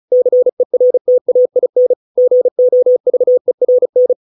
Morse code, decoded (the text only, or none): CERTAIN GOVERN